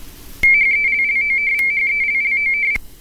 RTTY, about 45.45 baud. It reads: HELLOVHELLOV